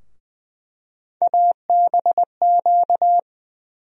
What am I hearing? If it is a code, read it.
Morse code, heard as a keyed tone, about 20 wpm, 700 Hz. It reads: ABQ
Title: USA Call Sign Postfixes - 3 Letters 20wpm